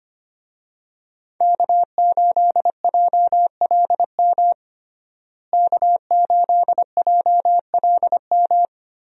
Morse code, heard as a keyed tone, about 25 wpm, 700 Hz. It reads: K8JLM K8JLM
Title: USA Call Signs - 5 Characters - Repeated 2x - 25wpm